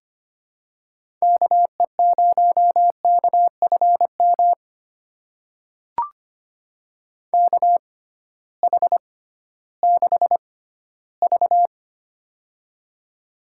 Morse code, heard as a keyed tone, about 25 wpm, spaced as for 14 wpm, 700 Hz. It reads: KE0KFM KH6V
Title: USA Call Signs - 6x Character Spacing - 25wpm